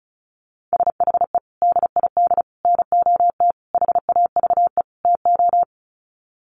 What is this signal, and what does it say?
Morse code, heard as a keyed tone, about 34 wpm, 700 Hz. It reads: SHE DID NOT HAVE TO
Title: Sets of 5 Words Encore2 35wpm